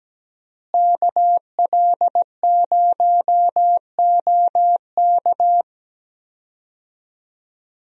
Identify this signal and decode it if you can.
Morse code, heard as a keyed tone, about 17 wpm, 700 Hz. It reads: KL0OK
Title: USA Call Signs - 5 Characters - Repeated 2x - 17wpm